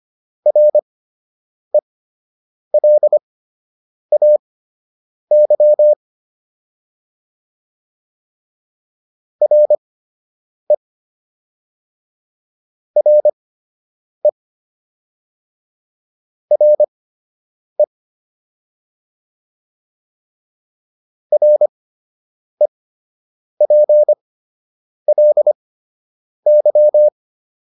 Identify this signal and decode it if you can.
Morse code, heard as a keyed tone, about 25 wpm, spaced as for 8 wpm, 600 Hz. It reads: RELAY RE RE RE REPLY